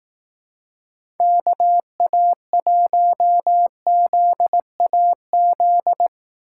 Morse code, heard as a keyed tone, about 18 wpm, 700 Hz. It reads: KA1ZAZ
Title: Speed Racing - USA Call Signs 15wpm